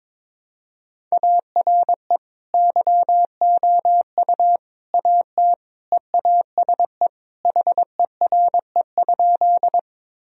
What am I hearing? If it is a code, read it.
Morse code, heard as a keyed tone, about 22 wpm, 700 Hz. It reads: ARE YOU AT EASE HERE?
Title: Sentences from Top 1000 Words - Review Part 1 - 22wpm